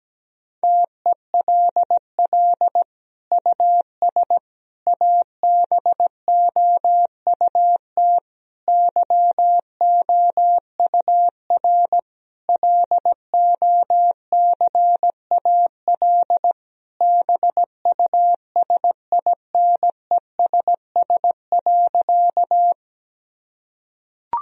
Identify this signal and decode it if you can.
Morse code, heard as a keyed tone, about 17 wpm, 700 Hz. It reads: TELL US ABOUT YOUR LOCAL BUSINESS.
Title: Sentences from Top 1000 Words - Review Part 3 - 17wpm